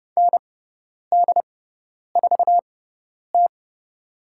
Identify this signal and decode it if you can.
Morse code, heard as a keyed tone, about 30 wpm, spaced as for 10 wpm, 700 Hz. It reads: ND4T
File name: Lesson 17g - Letter u - Call Signs 30wpm